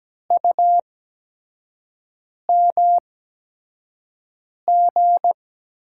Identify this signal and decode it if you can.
Morse code, heard as a keyed tone, about 17 wpm, spaced as for 5 wpm, 700 Hz. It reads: UMG
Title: USA Call Signs - 6 Characters - 8x Character Spacing - 17wpm